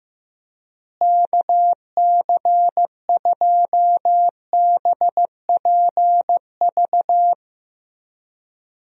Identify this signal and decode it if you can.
Morse code, heard as a keyed tone, about 15 wpm, 700 Hz. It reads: KC2BPV
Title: USA Call Signs - 6 Characters - Repeated 2x - 15wpm